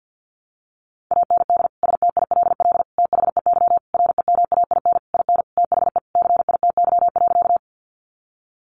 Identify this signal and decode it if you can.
Morse code, heard as a keyed tone, about 50 wpm, 700 Hz. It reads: AND STILL THEY REMAIN IN THE CITY.
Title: Sentences from Top 1200 Words 50wpm